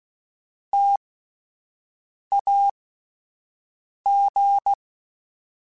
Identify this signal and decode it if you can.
Morse code, heard as a keyed tone, about 16 wpm, spaced as for 5 wpm, 800 Hz. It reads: TAG